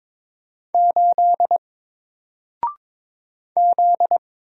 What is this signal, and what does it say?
Morse code, heard as a keyed tone, about 22 wpm, 700 Hz. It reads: 8 Z
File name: Single-Letter-Number-22wpm